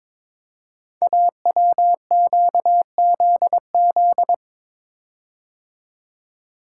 Morse code, heard as a keyed tone, about 22 wpm, 700 Hz. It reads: AWQZZ
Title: Code Groups - 5 Letters - 22wpm